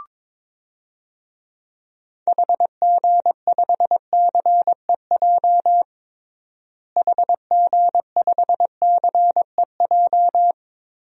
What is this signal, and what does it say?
Morse code, heard as a keyed tone, about 21 wpm, 700 Hz. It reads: HG5CEJ HG5CEJ
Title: DXCC-100 Call Signs - 6 Characters - Repeated 3x - 22wpm